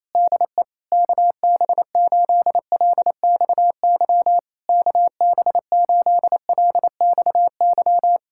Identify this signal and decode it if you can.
Morse code, heard as a keyed tone, about 28 wpm, 700 Hz. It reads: DE KB8LXY KB8LXY
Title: Speed-Racing Common QSO Phrases Encore - 28wpm